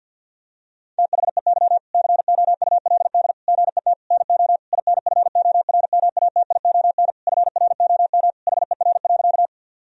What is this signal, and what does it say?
Morse code, heard as a keyed tone, about 50 wpm, 700 Hz. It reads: THEY COULD GET NO INFORMATION FROM HER.
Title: Sentences from Top 1000 Words - Review Part 3 - 50wpm